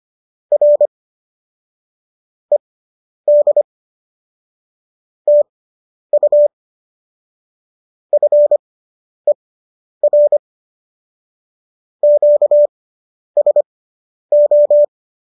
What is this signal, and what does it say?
Morse code, heard as a keyed tone, about 25 wpm, spaced as for 10 wpm, 590 Hz. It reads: R ED TU FER QSO